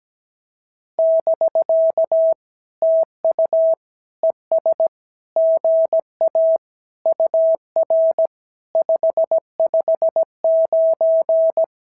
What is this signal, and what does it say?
Morse code, heard as a keyed tone, about 17 wpm, 650 Hz. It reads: <BK> TU ES GA UR 559
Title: SOTA QSO Practice 17wpm